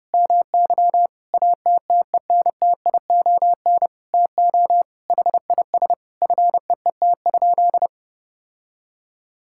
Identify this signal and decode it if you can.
Morse code, heard as a keyed tone, about 30 wpm, 700 Hz. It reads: MY ATTENTION TO HIS FEET?
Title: Sentences from Top 1000 Words - Review Part 6 - 30wpm